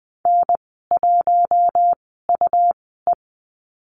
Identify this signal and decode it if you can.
Morse code, heard as a keyed tone, about 19 wpm, spaced as for 14 wpm, 700 Hz. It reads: N1UE
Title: 167 Lesson 27g - Number 7 - Call Signs 20wpm